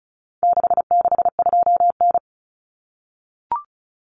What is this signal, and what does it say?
Morse code, heard as a keyed tone, about 34 wpm, 700 Hz. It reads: 662N
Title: Common Cut Numbers - 4-Digit 35wpm